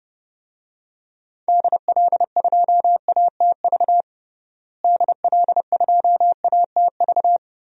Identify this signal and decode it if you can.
Morse code, heard as a keyed tone, about 30 wpm, 700 Hz. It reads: DL2ATV DL2ATV
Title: DXCC-100 Call Signs - 6 Characters - Repeated 3x - 30wpm